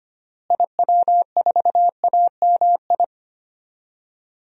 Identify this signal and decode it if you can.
Morse code, heard as a keyed tone, about 25 wpm, 700 Hz. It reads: IW4AMI
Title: DXCC-100 Call Signs - 6 Characters - Repeated 3x - 25wpm